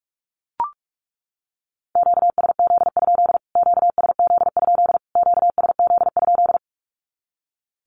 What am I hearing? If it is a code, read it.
Morse code, heard as a keyed tone, about 44 wpm, 700 Hz. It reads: QSZ? QSZ? QSZ?
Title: Q-Codes Repeated 3x 45wpm